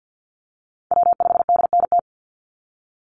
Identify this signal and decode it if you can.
Morse code, heard as a keyed tone, about 49 wpm, 700 Hz. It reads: W5DNT
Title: Call Signs Repeated 3x - 50wpm